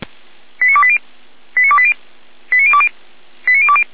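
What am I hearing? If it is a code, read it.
Five-tone selective call (ZVEI), heard as quick runs of five tones, tones of about 70 ms. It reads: 89280 89280 89020 89020
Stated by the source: ZVEI five-tone selective call received via SDR, from the Feuerwehr Orlamünde fire brigade audio archive.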